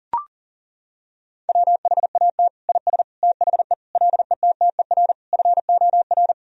Morse code, heard as a keyed tone, about 37 wpm, 700 Hz. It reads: WHAT IS THE LETTER FOR
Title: Sentences from Top 1000 Words - Review Part 2 - 40wpm